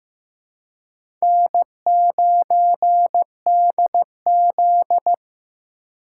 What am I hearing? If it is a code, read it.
Morse code, heard as a keyed tone, about 15 wpm, 700 Hz. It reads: N9DZ
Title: USA Call Signs - 4 Characters - 2x Character Spacing - 15wpm